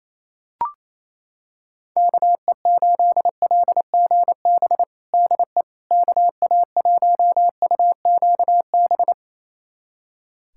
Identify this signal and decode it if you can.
Morse code, heard as a keyed tone, about 28 wpm, 700 Hz. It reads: KE8LGB DE KA1UQB
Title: USA Callsign DE Callsign - 6 Characters - 28wpm